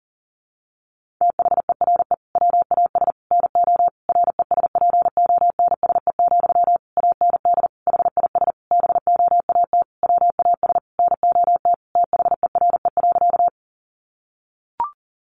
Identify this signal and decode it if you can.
Morse code, heard as a keyed tone, about 40 wpm, 700 Hz. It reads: THERE WAS NO RESPONSE, AND HIS BOAT WAS NOT THERE.
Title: Sentences from Top 1400 Words 40wpm